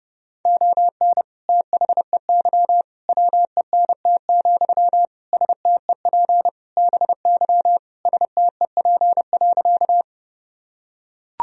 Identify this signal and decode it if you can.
Morse code, heard as a keyed tone, about 32 wpm, 700 Hz. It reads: ON THEY WENT, STEP BY STEP.
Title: Sentences from Top 1000 Words - Review Part 2 - 30wpm